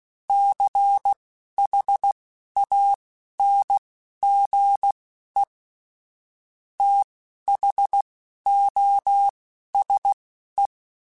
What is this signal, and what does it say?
Morse code, heard as a keyed tone, about 16 wpm, spaced as for 12 wpm, 800 Hz. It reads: CHANGE THOSE